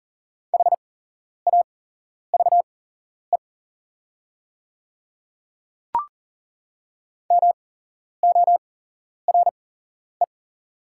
Morse code, heard as a keyed tone, about 40 wpm, spaced as for 11 wpm, 700 Hz. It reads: HAVE MORE
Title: 8x Character Spacing - Top 100 Words - 40wpm